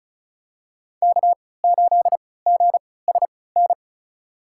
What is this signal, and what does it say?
Morse code, heard as a keyed tone, about 36 wpm, spaced as for 20 wpm, 700 Hz. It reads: K8GSN
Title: USA Call Signs - 3x Character Spacing - 35wpm